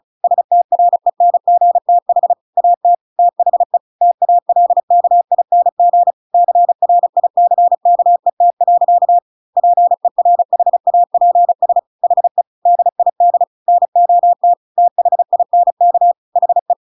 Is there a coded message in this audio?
Yes: Morse code from 0.2 to 16.7 s, heard as a keyed tone, about 34 wpm, 700 Hz. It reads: STRENGTH AT THE TALKING CRICKET. PERHAPS HE DID NOT THINK HE